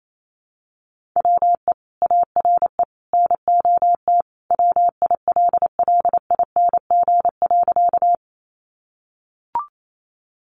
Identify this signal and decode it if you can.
Morse code, heard as a keyed tone, about 28 wpm, 700 Hz. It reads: WE ARE NOT WILLING.
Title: Sentences from Top 1000 Words 28wpm